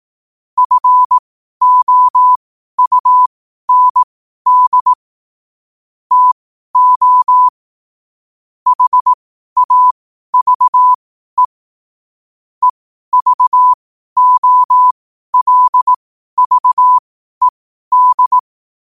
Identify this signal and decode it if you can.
Morse code, heard as a keyed tone, about 18 wpm, spaced as for 12 wpm, 1000 Hz. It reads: FOUND TO HAVE EVOLVED
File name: qotdfiles / 12wpm